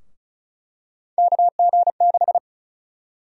Morse code, heard as a keyed tone, about 34 wpm, 700 Hz. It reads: KG6
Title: USA Call Sign Prefixes - 35wpm